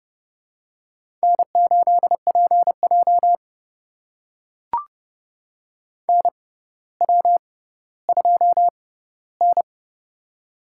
Morse code, heard as a keyed tone, about 31 wpm, spaced as for 16 wpm, 700 Hz. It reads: N8PJ NW2N
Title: USA Call Signs - 4 Characters - 6x Character Spacing - 30wpm